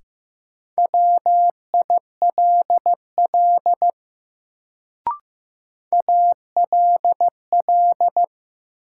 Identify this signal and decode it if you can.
Morse code, heard as a keyed tone, about 15 wpm, 700 Hz. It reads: WILL ALL
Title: Sets of 2 Words - New Word Drill - 15wpm